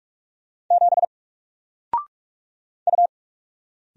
Morse code, heard as a keyed tone, about 46 wpm, 700 Hz. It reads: 7 U
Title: Single Character (Letter:Number:Punctuation:Prosign) Alt 45wpm